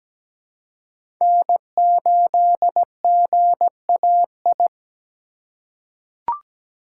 Morse code, heard as a keyed tone, about 17 wpm, 700 Hz. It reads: N8GAI
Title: USA Call Signs - 2x Character Spacing - 17wpm